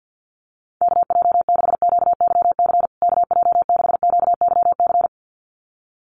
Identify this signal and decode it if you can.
Morse code, heard as a keyed tone, about 50 wpm, 700 Hz. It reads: KJ6QYC KJ6QYC
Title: USA Call Signs Repeated 2x 50wpm